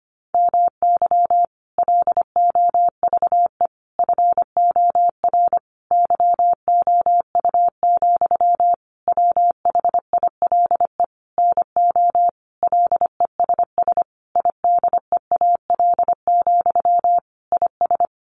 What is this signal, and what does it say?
Morse code, heard as a keyed tone, about 25 wpm, 700 Hz. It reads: MY LOVE FOR YOU, WHILE NO LESS IDEAL, IS
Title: Sentences from Top 1400 Words 25wpm